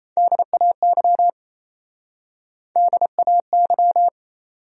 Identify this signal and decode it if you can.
Morse code, heard as a keyed tone, about 37 wpm, 700 Hz. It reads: DAY DAY